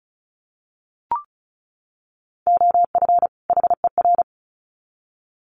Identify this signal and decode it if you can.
Morse code, heard as a keyed tone, about 34 wpm, 700 Hz. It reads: OF HER
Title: Sets of 2 Words 35wpm